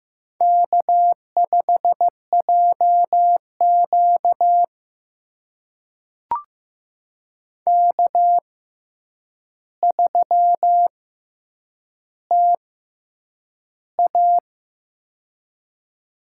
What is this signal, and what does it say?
Morse code, heard as a keyed tone, about 15 wpm, spaced as for 8 wpm, 700 Hz. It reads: K5JQ K3TA
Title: USA Call Signs - 4 Characters - 6x Character Spacing - 15wpm